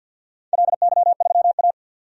Morse code, heard as a keyed tone, about 50 wpm, 700 Hz. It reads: LY3A
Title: DXCC-100 Call Signs - 4 Characters - 50wpm